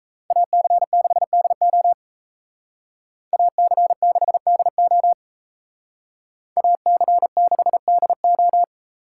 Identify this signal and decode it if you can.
Morse code, heard as a keyed tone, about 32 wpm, 700 Hz. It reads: AC6DO AC6DO AC6DO
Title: Speed Racing - USA 2x2 Call Signs 28wpm